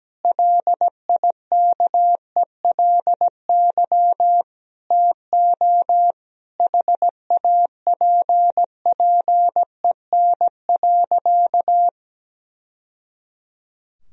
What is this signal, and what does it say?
Morse code, heard as a keyed tone, about 17 wpm, 700 Hz. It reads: LIKELY TO HAPPEN.
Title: Sentences from Top 1000 Words - Review Part 4 - 17wpm